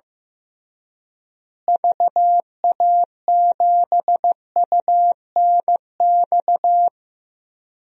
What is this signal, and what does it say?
Morse code, heard as a keyed tone, about 16 wpm, 700 Hz. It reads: VA7UNX
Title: DXCC-100 Call Signs - 6 Characters - Repeated 2x - 15wpm